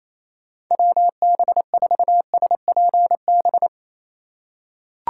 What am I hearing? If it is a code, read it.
Morse code, heard as a keyed tone, about 28 wpm, 700 Hz. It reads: WB4SPB
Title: K1USN SST QSO Practice - 28wpm